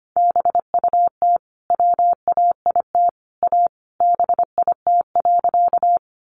Morse code, heard as a keyed tone, about 25 wpm, 700 Hz. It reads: BUT WAIT A BIT.
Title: Sentences from Top 900 Words 25wpm